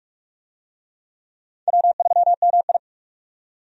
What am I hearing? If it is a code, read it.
Morse code, heard as a keyed tone, about 46 wpm, 700 Hz. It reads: W3MI
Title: USA Call Signs - 4 Characters - 45wpm